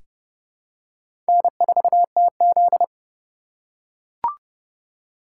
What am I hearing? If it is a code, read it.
Morse code, heard as a keyed tone, about 32 wpm, 700 Hz. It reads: N4TZ
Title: K1USN SST QSO Practice - 30wpm